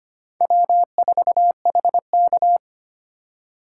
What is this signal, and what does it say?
Morse code, heard as a keyed tone, about 25 wpm, 700 Hz. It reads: W4HK
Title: USA Call Signs - 4 Characters - 4x Character Spacing - 25wpm